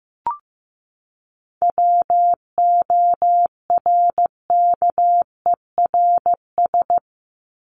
Morse code, heard as a keyed tone, about 15 wpm, 700 Hz. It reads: WORKERS
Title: ES:S Suffix - Top 100 Words with Base Words 15wpm